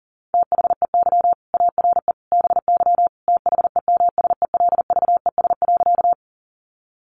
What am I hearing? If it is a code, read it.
Morse code, heard as a keyed tone, about 40 wpm, 690 Hz. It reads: THEY ARE BY THEMSELVES.
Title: Sentences from Top 800 Words 40wpm